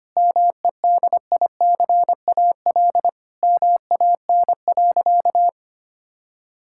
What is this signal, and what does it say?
Morse code, heard as a keyed tone, about 25 wpm, 700 Hz. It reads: MEDICAL MAN.
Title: Sentences from Top 1000 Words - Review Part 5 - 25wpm